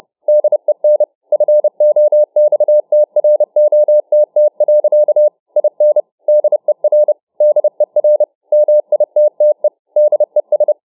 Morse code, heard as a keyed tone, about 30 wpm, 600 Hz. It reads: DEN FOXTROTT. IN DER DER MITTE DES